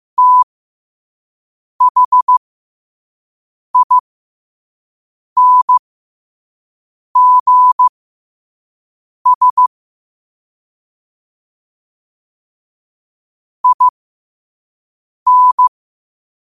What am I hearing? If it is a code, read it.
Morse code, heard as a keyed tone, about 15 wpm, spaced as for 5 wpm, 1000 Hz. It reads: THINGS IN
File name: qotdfiles / 5wpm